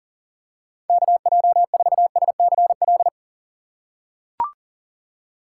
Morse code, heard as a keyed tone, about 40 wpm, 700 Hz. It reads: KJ4SCL
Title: USA Call Signs - 6 Characters - 4x Character Spacing - 40wpm